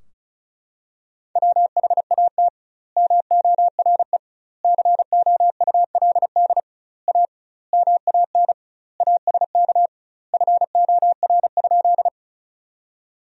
Morse code, heard as a keyed tone, about 36 wpm, 700 Hz. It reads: WHAT MORE COULD A MAN ASK FOR?
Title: 2x Word-Spacing - Sentences from Top 200 Words - 35wpm